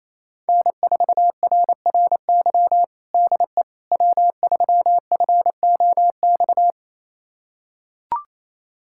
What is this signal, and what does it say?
Morse code, heard as a keyed tone, about 28 wpm, 700 Hz. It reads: N4RRY DE W3FOX
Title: USA Callsign DE Callsign - 5 Characters - 28wpm